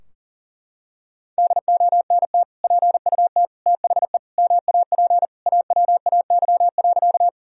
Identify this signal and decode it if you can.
Morse code, heard as a keyed tone, about 37 wpm, 700 Hz. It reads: DONT PUT THE MAP AWAY.
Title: Sentences from Top 1000 Words - Review Part 2 - 40wpm